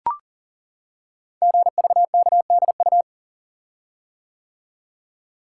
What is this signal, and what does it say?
Morse code, heard as a keyed tone, about 40 wpm, 700 Hz. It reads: GVKDU
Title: Code Groups - 5 Letter:Numbers - 40wpm